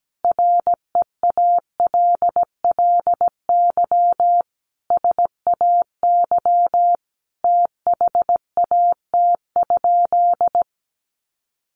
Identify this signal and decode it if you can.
Morse code, heard as a keyed tone, about 17 wpm, 700 Hz. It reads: REALLY SAY THAT?
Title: Sentences from Top 1500 Words 17wpm